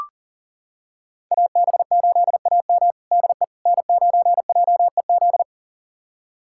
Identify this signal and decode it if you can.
Morse code, heard as a keyed tone, about 40 wpm, 700 Hz. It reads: AB8AM DE N9JEZ
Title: USA Callsign DE Callsign - 5 Characters - 40wpm